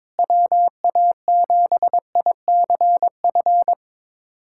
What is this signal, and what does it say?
Morse code, heard as a keyed tone, about 22 wpm, 700 Hz. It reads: WA7ICF
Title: USA Callsign DE Callsign - 6 Characters - 22wpm